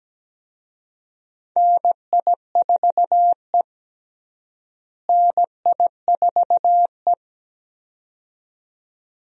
Morse code, heard as a keyed tone, about 17 wpm, 700 Hz. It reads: NI4E NI4E
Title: USA Call Signs - 4 Characters - Repeated 2x - 17wpm